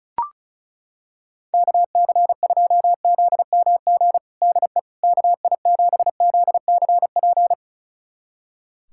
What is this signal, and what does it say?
Morse code, heard as a keyed tone, about 34 wpm, 700 Hz. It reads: KC2ZMG DE KI7ZCP
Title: USA Callsign DE Callsign - 6 Characters - 35wpm